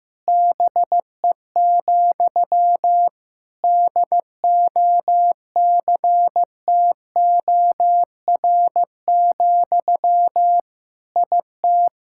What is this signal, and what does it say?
Morse code, heard as a keyed tone, about 16 wpm, 700 Hz. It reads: BE, DOCTOR, IT
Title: Sentences from Top 1600 Words 15wpm